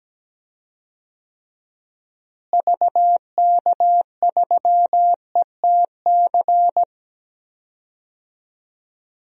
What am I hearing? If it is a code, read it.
Morse code, heard as a keyed tone, about 17 wpm, 700 Hz. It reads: VK3ETC
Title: DXCC-100 Call Signs - 6 Characters - Repeated 3x - 17wpm